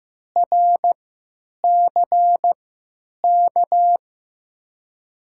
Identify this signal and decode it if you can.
Morse code, heard as a keyed tone, about 15 wpm, spaced as for 9 wpm, 700 Hz. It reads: RCK
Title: USA Call Signs - 6 Characters - 3x Character Spacing - 15wpm